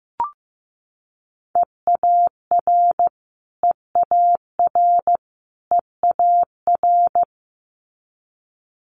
Morse code, heard as a keyed tone, about 15 wpm, 700 Hz. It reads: EAR EAR EAR
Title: Multi-letter Phonemes Repeated 3x 15wpm